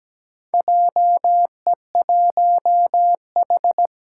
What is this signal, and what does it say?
Morse code, heard as a keyed tone, about 17 wpm, 700 Hz. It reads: JE1H
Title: Four Challenging Letter-Number Combinations 17wpm